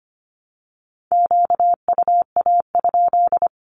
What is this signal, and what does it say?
Morse code, heard as a keyed tone, about 25 wpm, 700 Hz. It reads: QUA?
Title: Q-Codes with Spoken Meaning - 25wpm